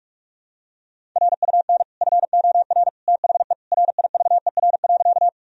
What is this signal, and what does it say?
Morse code, heard as a keyed tone, about 46 wpm, 700 Hz. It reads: RUN FOR THE RIVER.
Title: Sentences from Top 1000 Words - Review Part 1 - 45wpm